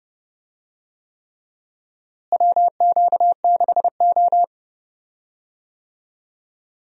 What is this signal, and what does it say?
Morse code, heard as a keyed tone, about 32 wpm, 700 Hz. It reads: WQ6O
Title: USA Call Signs - 4 Characters - Repeated 3x - 30wpm